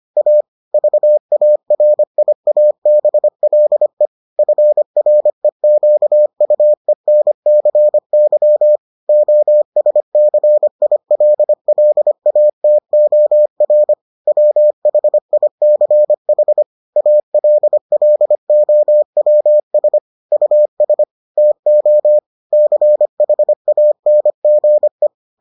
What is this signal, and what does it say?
Morse code, heard as a keyed tone, about 25 wpm, 600 Hz. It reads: A VARIABLE FREQUENCY OSCILLATOR WHICH ALLOWS US TO CHANGE